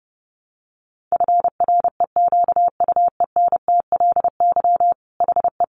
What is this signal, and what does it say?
Morse code, heard as a keyed tone, about 30 wpm, 700 Hz. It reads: FREQUENTLY HE
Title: Sentences from Top 1300 Words 30wpm